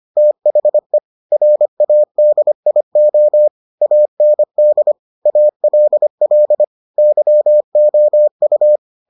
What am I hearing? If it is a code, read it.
Morse code, heard as a keyed tone, about 25 wpm, 600 Hz. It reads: THE RADIO AND ALL YOU